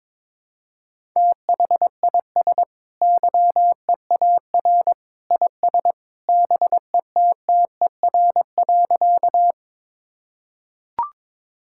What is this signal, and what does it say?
Morse code, heard as a keyed tone, about 22 wpm, 700 Hz. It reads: THIS YEAR IS BETTER.
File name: practice / 22wpm / Sentences with 2-Word Phrases - 22wpm